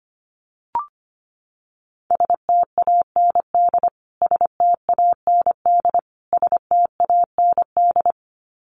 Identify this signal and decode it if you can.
Morse code, heard as a keyed tone, about 25 wpm, 700 Hz. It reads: STAND STAND STAND
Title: Top 500 Words Repeated 25wpm